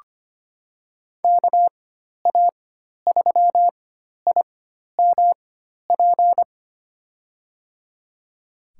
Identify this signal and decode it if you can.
Morse code, heard as a keyed tone, about 25 wpm, spaced as for 12 wpm, 700 Hz. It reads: KA3IMP
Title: USA Call Signs - 6 Characters - 4x Character Spacing - 25wpm